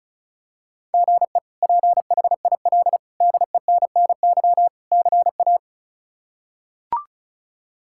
Morse code, heard as a keyed tone, about 34 wpm, 700 Hz. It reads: GE PHIL DENNY CA
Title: K1USN SST QSO Practice - 35wpm